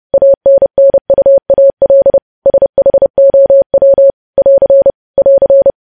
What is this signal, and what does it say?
Morse code, heard as a keyed tone, about 30 wpm, 550 Hz. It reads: ANNUAL SHOW <AR> <AR>
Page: News Headlines in Morse Code at 30 WPM